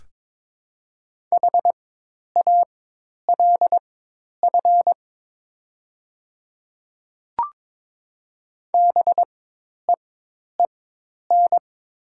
Morse code, heard as a keyed tone, about 22 wpm, spaced as for 10 wpm, 700 Hz. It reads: HALF BEEN
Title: 4x Character Spacing - Top 500 Words - 22wpm